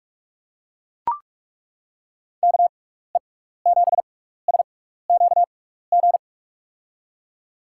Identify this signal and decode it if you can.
Morse code, heard as a keyed tone, about 42 wpm, spaced as for 15 wpm, 700 Hz. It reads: KE7SQG
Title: USA Call Signs - 6x Character Spacing - 45wpm